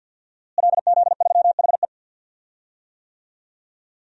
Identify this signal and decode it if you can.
Morse code, heard as a keyed tone, about 46 wpm, 700 Hz. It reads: LZ3HE
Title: DXCC-100 Call Signs - 5 Characters - Repeated 2x - 50wpm